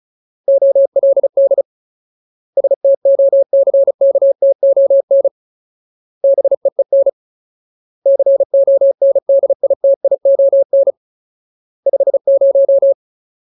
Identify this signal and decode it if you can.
Morse code, heard as a keyed tone, about 34 wpm, 550 Hz. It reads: OLD STOCKTON BEEN CONDITION 50
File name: Ultimate Word List / 35WPM Ultimate